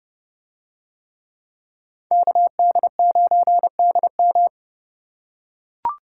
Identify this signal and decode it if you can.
Morse code, heard as a keyed tone, about 27 wpm, 700 Hz. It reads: KD9DM